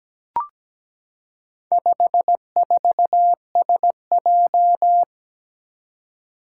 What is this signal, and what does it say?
Morse code, heard as a keyed tone, about 17 wpm, 700 Hz. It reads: 54SJ
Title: Four Challenging Letter-Number Combinations 17wpm